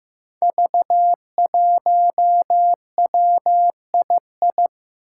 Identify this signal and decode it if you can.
Morse code, heard as a keyed tone, about 15 wpm, 700 Hz. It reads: V1WII
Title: Five Challenging Letter-Number Combinations 15wpm